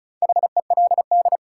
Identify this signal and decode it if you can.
Morse code, heard as a keyed tone, about 36 wpm, 700 Hz. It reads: HELD
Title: Sentences from Top 1600 Words - New Word Drill Alt - 35wpm